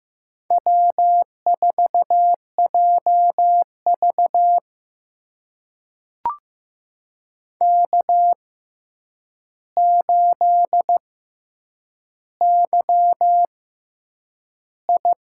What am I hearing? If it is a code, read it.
Morse code, heard as a keyed tone, about 15 wpm, spaced as for 8 wpm, 700 Hz. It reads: W4JV K8YI
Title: USA Call Signs - 4 Characters - 6x Character Spacing - 15wpm